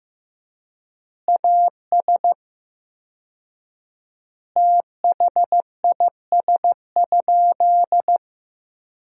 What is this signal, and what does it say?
Morse code, heard as a keyed tone, about 15 wpm, 700 Hz. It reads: AS THIS?
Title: 4x Word-Spacing - Sentences from Top 400 Words - 15wpm